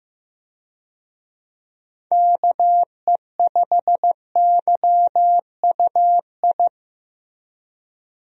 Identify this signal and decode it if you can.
Morse code, heard as a keyed tone, about 15 wpm, 700 Hz. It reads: KE5YUI
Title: USA Call Signs - 6 Characters - Repeated 2x - 15wpm